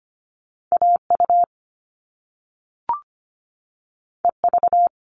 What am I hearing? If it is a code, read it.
Morse code, heard as a keyed tone, about 24 wpm, 700 Hz. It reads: AU EV
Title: Top Two Letter Combinations in Words 25wpm